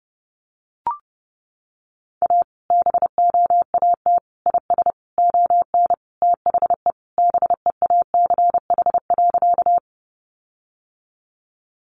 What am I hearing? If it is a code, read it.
Morse code, heard as a keyed tone, about 30 wpm, 700 Hz. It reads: A BOAT IS ON THE BEACH.
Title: Sentences from Top 1500 Words 30wpm